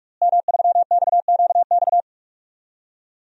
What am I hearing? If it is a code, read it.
Morse code, heard as a keyed tone, about 46 wpm, 700 Hz. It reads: M3XQX
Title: DXCC-100 Call Signs - 5 Characters - Repeated 3x - 45wpm